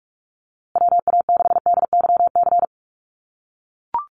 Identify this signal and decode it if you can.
Morse code, heard as a keyed tone, about 44 wpm, 690 Hz. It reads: WA6DYC
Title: USA Call Signs Repeated 3x 45wpm